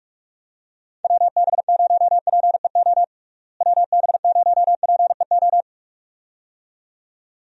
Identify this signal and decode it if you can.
Morse code, heard as a keyed tone, about 46 wpm, 700 Hz. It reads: WB0PEO WB0PEO
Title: USA Call Signs - 6 Characters - Repeated 2x - 45wpm